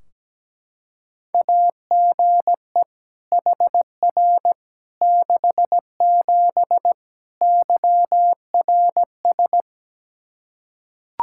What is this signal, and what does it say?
Morse code, heard as a keyed tone, about 17 wpm, 700 Hz. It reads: AGE HR 67 YRS
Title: K1USN SST QSO Practice 17wpm